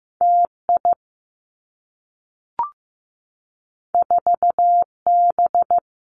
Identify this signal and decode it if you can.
Morse code, heard as a keyed tone, about 15 wpm, 700 Hz. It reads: T I 4 B
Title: Two Challenging Letter-Number Combinations 15wpm